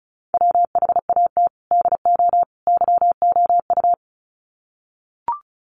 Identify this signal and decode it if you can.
Morse code, heard as a keyed tone, about 34 wpm, 700 Hz. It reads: WHAT DO YOU
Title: Sets of 3 Words 35wpm